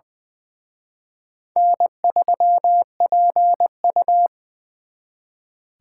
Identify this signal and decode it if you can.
Morse code, heard as a keyed tone, about 24 wpm, 700 Hz. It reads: N3PU
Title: Speed Racing - USA 1x2 Call Signs 20wpm